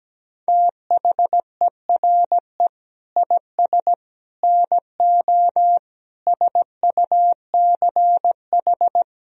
Morse code, heard as a keyed tone, about 17 wpm, 700 Hz. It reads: THERE IS NO SUCH
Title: Sentences from Top 1000 Words - Review Part 1 - 17wpm